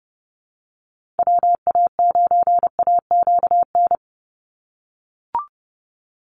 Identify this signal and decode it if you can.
Morse code, heard as a keyed tone, about 30 wpm, 700 Hz. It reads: WA9AQN
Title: Call Signs - 30wpm